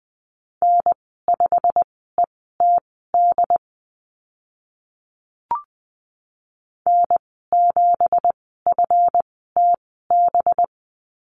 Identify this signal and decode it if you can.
Morse code, heard as a keyed tone, about 20 wpm, spaced as for 14 wpm, 700 Hz. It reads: N5ETD N7FTB
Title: 186 Lesson 30g - Letter b - Call Signs 20wpm